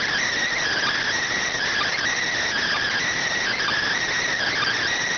Emergency Alert System (SAME) tones (header bursts), heard as harsh buzzing bursts. It is heard from 0.5 to 4.9 s.